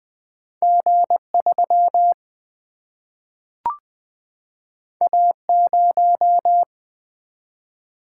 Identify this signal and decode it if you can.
Morse code, heard as a keyed tone, about 20 wpm, 700 Hz. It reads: G3 A0
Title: Single-Letter Single-Number 20wpm